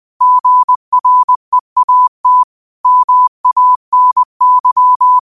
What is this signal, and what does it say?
Morse code, heard as a keyed tone, about 20 wpm, 1000 Hz. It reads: GREAT MANY